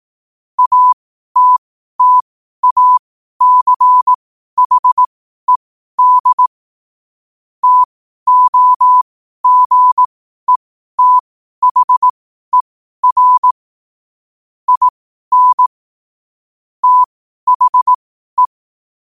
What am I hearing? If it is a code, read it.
Morse code, heard as a keyed tone, about 18 wpm, spaced as for 12 wpm, 1000 Hz. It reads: ATTACHED TOGETHER IN THE
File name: qotdfiles / 12wpm